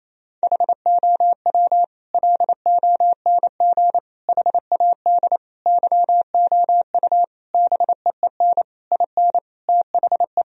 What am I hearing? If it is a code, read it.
Morse code, heard as a keyed tone, about 28 wpm, 700 Hz. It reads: HOW LONG HAD YOU BEEN IN THE
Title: Sentences from Top 1000 Words - Review Part 3 - 28wpm